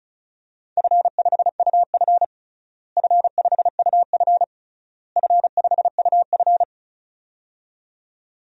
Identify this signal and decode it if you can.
Morse code, heard as a keyed tone, about 36 wpm, 700 Hz. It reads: F5UF F5UF F5UF
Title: DXCC-100 Call Signs - 4 Characters - Repeated 3x - 35wpm